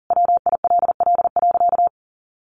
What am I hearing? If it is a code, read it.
Morse code, heard as a keyed tone, about 40 wpm, 700 Hz. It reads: WILL.
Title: Sentences from Top 1000 Words 40wpm